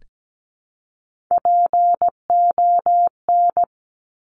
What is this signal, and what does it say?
Morse code, heard as a keyed tone, about 17 wpm, 700 Hz. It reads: PON
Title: Top Three Letter Combinations in Words 17wpm